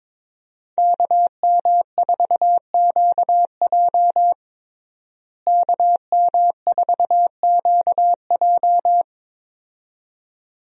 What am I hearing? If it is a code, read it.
Morse code, heard as a keyed tone, about 22 wpm, 700 Hz. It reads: KM4QJ KM4QJ
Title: USA Call Signs - 5 Characters - Repeated 2x - 22wpm